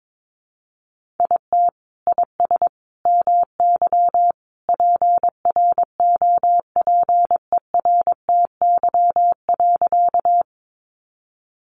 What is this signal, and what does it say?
Morse code, heard as a keyed tone, about 22 wpm, 700 Hz. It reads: IT IS MY PROPERTY.
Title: Sentences from Top 1000 Words 22wpm